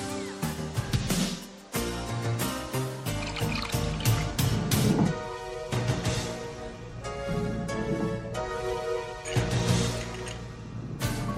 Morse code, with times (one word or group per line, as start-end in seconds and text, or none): none